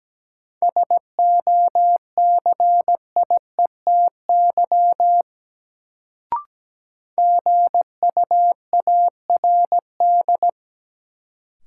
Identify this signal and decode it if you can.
Morse code, heard as a keyed tone, about 17 wpm, 700 Hz. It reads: SOCIETY GUARD
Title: Sentences from Top 700 Words - New Word Drill - 17wpm